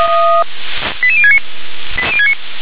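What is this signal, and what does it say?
Five-tone selective call (ZVEI), heard as quick runs of five tones, tones of about 70 ms. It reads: 88069
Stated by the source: ZVEI five-tone call received by SDR.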